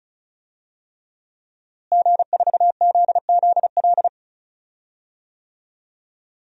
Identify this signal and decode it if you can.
Morse code, heard as a keyed tone, about 34 wpm, 700 Hz. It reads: G4ZZL
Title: DXCC-100 Call Signs - 5 Characters - Repeated 3x - 35wpm